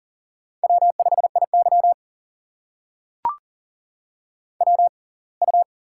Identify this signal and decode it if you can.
Morse code, heard as a keyed tone, about 38 wpm, spaced as for 27 wpm, 700 Hz. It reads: W5IY WU
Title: USA Call Signs - 4 Characters - 6x Character Spacing - 40wpm